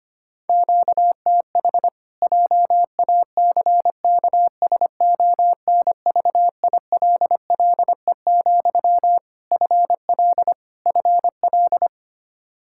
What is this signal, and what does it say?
Morse code, heard as a keyed tone, about 25 wpm, 700 Hz. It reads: QTH JACKSONVILLE, FL FL
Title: Practice QSO - Exchange 1 - Beginning - 25wpm